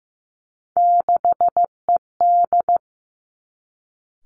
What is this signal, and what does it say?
Morse code, heard as a keyed tone, about 15 wpm, 700 Hz. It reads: 6ED
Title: Three Challenging Letter-Number Combinations 15wpm